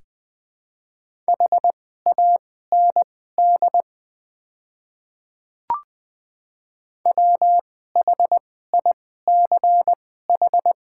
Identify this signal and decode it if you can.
Morse code, heard as a keyed tone, about 20 wpm, spaced as for 14 wpm, 700 Hz. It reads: HAND WHICH
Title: 2x Character Spacing - Top 200 Words - 20wpm